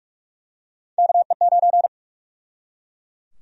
Morse code, heard as a keyed tone, about 46 wpm, 700 Hz. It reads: KE9
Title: USA Call Sign Prefixes - 45wpm